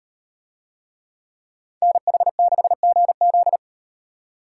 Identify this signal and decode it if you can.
Morse code, heard as a keyed tone, about 37 wpm, 700 Hz. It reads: NH6GZ